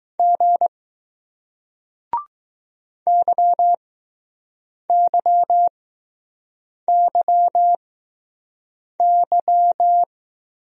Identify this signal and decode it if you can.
Morse code, heard as a keyed tone, about 20 wpm, spaced as for 6 wpm, 700 Hz. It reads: G YYYY